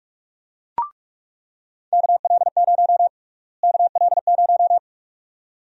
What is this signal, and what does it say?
Morse code, heard as a keyed tone, about 46 wpm, 700 Hz. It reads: KL0 KL0
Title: USA Call Sign Prefixes - Repeated 2x - 45wpm